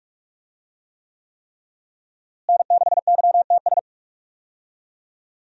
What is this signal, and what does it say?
Morse code, heard as a keyed tone, about 46 wpm, 700 Hz. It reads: N6YTS
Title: USA Call Signs - 5 Characters - Repeated 3x - 45wpm